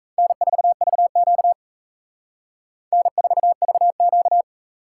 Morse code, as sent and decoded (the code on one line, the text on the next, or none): -. ....- ...- --.- / -. ....- ...- --.-
N4VQ N4VQ